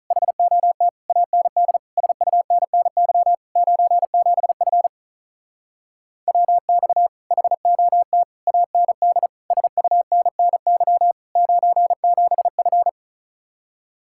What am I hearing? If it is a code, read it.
Morse code, heard as a keyed tone, about 43 wpm, 700 Hz. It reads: HOT AND SUNNY 97F WX HOT AND SUNNY 97F
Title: Speed-Racing Common QSO Phrases - 35wpm